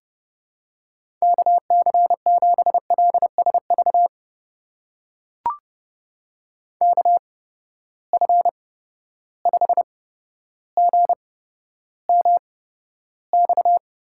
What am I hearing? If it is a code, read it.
Morse code, heard as a keyed tone, about 32 wpm, spaced as for 13 wpm, 700 Hz. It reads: KC7LSV KF5GMX
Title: USA Call Signs - 6 Characters - 8x Character Spacing - 30wpm